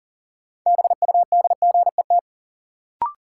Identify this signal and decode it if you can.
Morse code, heard as a keyed tone, about 40 wpm, 700 Hz. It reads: BUDGET
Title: Sentences from Top 1200 Words - New Word Drill - 40wpm